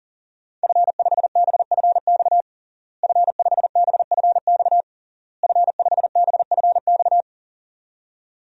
Morse code, heard as a keyed tone, about 37 wpm, 700 Hz. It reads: F5BFX F5BFX F5BFX
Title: DXCC-100 Call Signs - 5 Characters - Repeated 3x - 40wpm